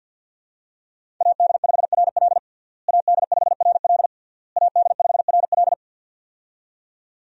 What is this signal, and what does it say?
Morse code, heard as a keyed tone, about 50 wpm, 700 Hz. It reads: AD5RL AD5RL AD5RL
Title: USA Call Signs - 5 Characters - Repeated 3x - 50wpm